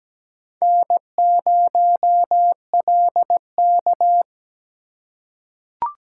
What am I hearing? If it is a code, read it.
Morse code, heard as a keyed tone, about 17 wpm, 700 Hz. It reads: N0LK
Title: USA Call Signs - 4 Characters - 4x Character Spacing - 17wpm